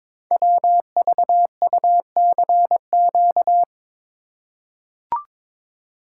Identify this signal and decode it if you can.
Morse code, heard as a keyed tone, about 22 wpm, 700 Hz. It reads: WVUCQ
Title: Code Groups - 5 Letters - 22wpm